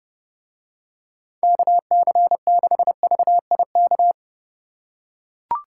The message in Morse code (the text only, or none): KC6VIK